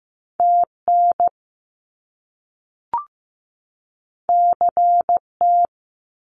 Morse code, heard as a keyed tone, about 15 wpm, 700 Hz. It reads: TN CT
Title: US States 15wpm